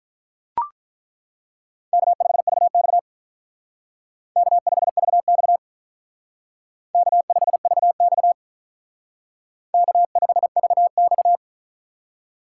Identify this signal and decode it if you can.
Morse code, heard as a keyed tone, about 42 wpm, 700 Hz. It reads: K5VX K5VX K5VX K5VX
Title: Speed Racing - USA 1x2 Call Signs 35wpm